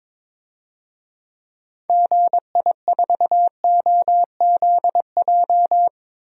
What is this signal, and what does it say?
Morse code, heard as a keyed tone, about 22 wpm, 700 Hz. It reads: GI4OZJ
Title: DXCC-100 Call Signs - 6 Characters - 22wpm